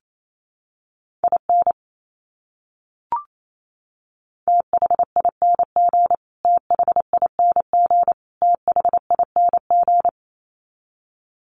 Morse code, heard as a keyed tone, about 28 wpm, 700 Hz. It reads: IN THING THING THING
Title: Top 200 Words Repeated 28wpm